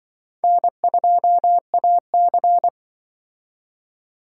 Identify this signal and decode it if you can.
Morse code, heard as a keyed tone, about 24 wpm, 700 Hz. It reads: N2AC
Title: Speed Racing - USA 1x2 Call Signs 20wpm